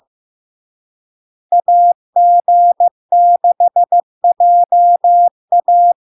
Morse code, heard as a keyed tone, about 15 wpm, 700 Hz. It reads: AG6JA
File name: LARK-FD-2022-Log-in-CW-15wpm-700-Hz